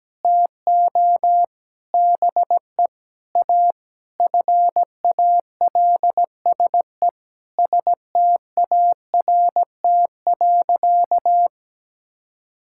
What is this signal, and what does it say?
Morse code, heard as a keyed tone, about 17 wpm, 700 Hz. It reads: TO BE A FALSE START.
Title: Sentences from Top 1600 Words 17wpm